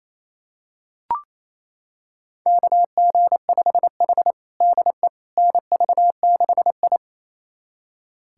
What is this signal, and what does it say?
Morse code, heard as a keyed tone, about 29 wpm, 700 Hz. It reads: KG5H DE NV6I
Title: USA Callsign DE Callsign - 4 Characters - 28wpm